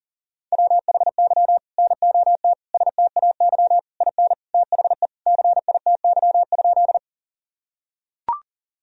Morse code, heard as a keyed tone, about 37 wpm, 700 Hz. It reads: WHY NOT STAY IN THE CITY?
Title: Sentences from Top 1000 Words - Review Part 1 - 40wpm